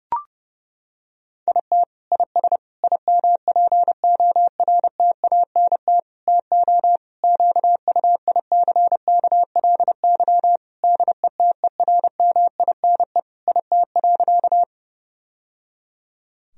Morse code, heard as a keyed tone, about 30 wpm, 700 Hz. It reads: IT IS IMPORTANT TO QUICKLY DETERMINE IT.
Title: Sentences from Top 1000 Words - Review Part 4 - 30wpm